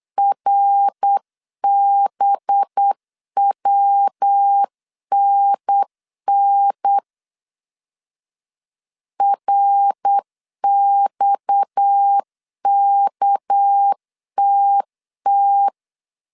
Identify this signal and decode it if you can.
Morse code, heard as a keyed tone, about 8 wpm, 800 Hz. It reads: RBWNN RXKTT
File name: CW / DET / 1 6 1500C S C 8wpm 5219